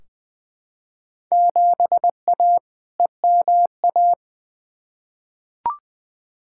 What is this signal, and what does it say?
Morse code, heard as a keyed tone, about 20 wpm, 700 Hz. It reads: 7A EMA
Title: Field Day Exchanges Encore - 20wpm